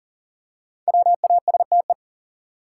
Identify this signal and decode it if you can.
Morse code, heard as a keyed tone, about 40 wpm, 700 Hz. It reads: WASTE
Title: Five Letter Words - 40wpm